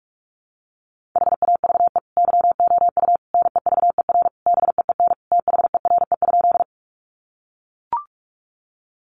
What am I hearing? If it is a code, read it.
Morse code, heard as a keyed tone, about 44 wpm, 700 Hz. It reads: HAVE YOU NEVER BEEN THERE?
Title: Sentences from Top 300 Words 45wpm